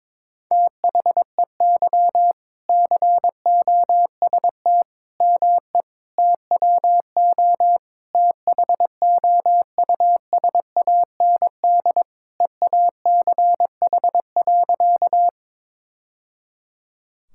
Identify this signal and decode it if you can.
Morse code, heard as a keyed tone, about 22 wpm, 700 Hz. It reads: THEY COST ME TWO THOUSAND EACH.
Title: Sentences from Top 1000 Words - Review Part 5 - 22wpm